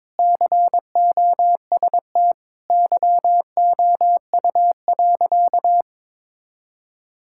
Morse code, heard as a keyed tone, about 21 wpm, 700 Hz. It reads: COST YOU.
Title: Sentences from Top 1000 Words - Review Part 5 - 22wpm